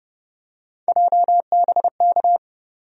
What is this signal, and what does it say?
Morse code, heard as a keyed tone, about 30 wpm, 700 Hz. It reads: JBK
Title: USA Call Sign Postfixes - 3 Letters 30wpm